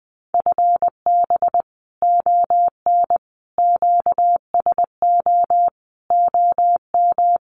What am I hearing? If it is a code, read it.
Morse code, heard as a keyed tone, about 20 wpm, 700 Hz. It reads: FB ON QSO OM
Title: Common QSO Phrases 20wpm